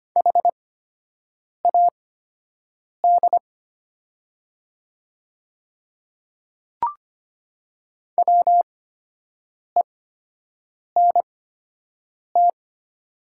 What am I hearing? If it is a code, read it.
Morse code, heard as a keyed tone, about 25 wpm, spaced as for 7 wpm, 700 Hz. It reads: HAD WENT
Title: 8x Character Spacing - Top 200 Words - 25wpm